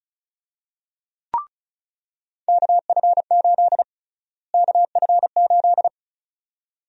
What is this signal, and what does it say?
Morse code, heard as a keyed tone, about 36 wpm, 700 Hz. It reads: KF8 KF8
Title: USA Call Sign Prefixes - Repeated 2x - 35wpm